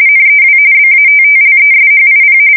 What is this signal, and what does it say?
RTTY, about 45.45 baud. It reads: NEW BATTLESHIP
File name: fsk170 / USS-New-Jersey